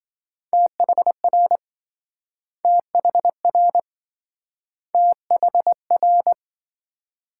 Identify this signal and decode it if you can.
Morse code, heard as a keyed tone, about 27 wpm, 700 Hz. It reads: THR THR THR